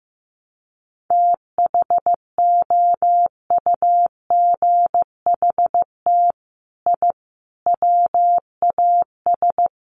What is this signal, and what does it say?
Morse code, heard as a keyed tone, about 15 wpm, 700 Hz. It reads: THOUGHT I WAS
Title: Sets of 5 Words Encore2 15wpm